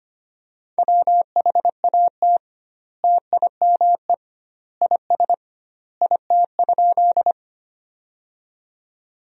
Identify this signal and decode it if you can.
Morse code, heard as a keyed tone, about 25 wpm, 700 Hz. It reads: WHAT TIME IS IT?
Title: 2x Word-Spacing - Sentences from Top 100 Words - 25wpm